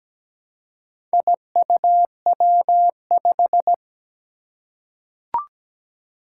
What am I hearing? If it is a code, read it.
Morse code, heard as a keyed tone, about 17 wpm, 700 Hz. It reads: IUW5
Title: Four Challenging Letter-Number Combinations 17wpm